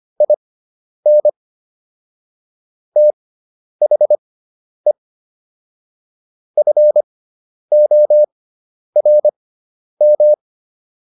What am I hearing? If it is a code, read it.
Morse code, heard as a keyed tone, about 25 wpm, spaced as for 10 wpm, 600 Hz. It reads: IN THE FORM